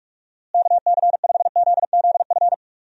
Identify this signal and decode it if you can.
Morse code, heard as a keyed tone, about 41 wpm, 700 Hz. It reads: KC5ZZF
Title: USA Call Signs - 6 Characters - Repeated 3x - 45wpm